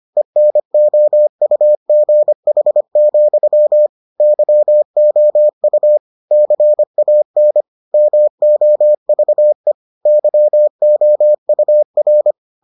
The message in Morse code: ENOUGH, YOU CAN MOVE YOUR

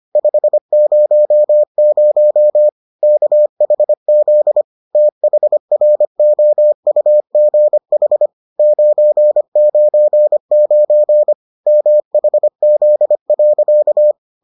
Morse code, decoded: 500 KHZ THROUGH 999 MHZ.

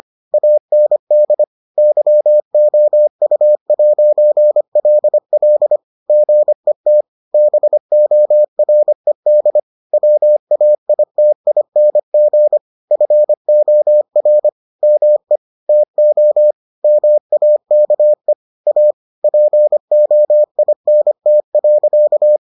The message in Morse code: AND YOU'LL GET BORED WAITING FOR ME TO MAKE A POINT.

